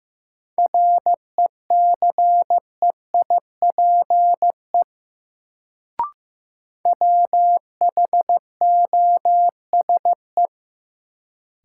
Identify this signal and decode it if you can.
Morse code, heard as a keyed tone, about 15 wpm, 700 Hz. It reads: RECEIPE WHOSE
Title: Sentences from Top 1200 Words - New Word Drill - 15wpm